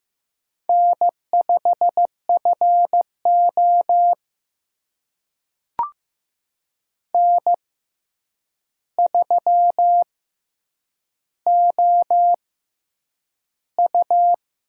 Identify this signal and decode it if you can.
Morse code, heard as a keyed tone, about 15 wpm, spaced as for 8 wpm, 700 Hz. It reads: N5FO N3OU
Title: USA Call Signs - 4 Characters - 6x Character Spacing - 15wpm